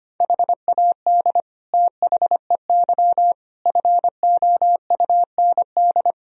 Morse code, heard as a keyed tone, about 25 wpm, 700 Hz. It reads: HAD THEY FOUND
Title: Sentences from Top 1000 Words - Review Part 2 - 25wpm